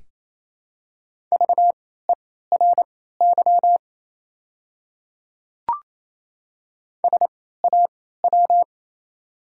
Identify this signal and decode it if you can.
Morse code, heard as a keyed tone, about 28 wpm, spaced as for 16 wpm, 700 Hz. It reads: VERY SAW
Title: 3x Character Spacing - Top 500 Words - 28wpm